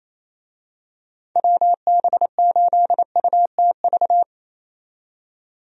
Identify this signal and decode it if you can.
Morse code, heard as a keyed tone, about 28 wpm, 700 Hz. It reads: WB8UTV
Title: USA Call Signs - 6 Characters - 28wpm